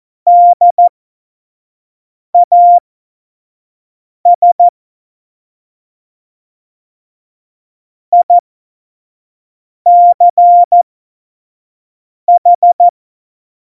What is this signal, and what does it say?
Morse code, heard as a keyed tone, about 14 wpm, spaced as for 5 wpm, 700 Hz. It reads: DAS ICH
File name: Morsekurs / Witze / Witz 05. Dezember 2017, 25 F70